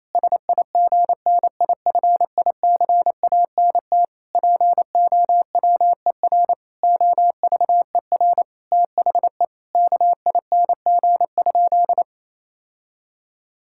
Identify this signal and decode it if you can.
Morse code, heard as a keyed tone, about 28 wpm, 700 Hz. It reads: SIGNIFICANT POWER OVER THE KING?
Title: Sentences from Top 1000 Words - Review Part 6 - 28wpm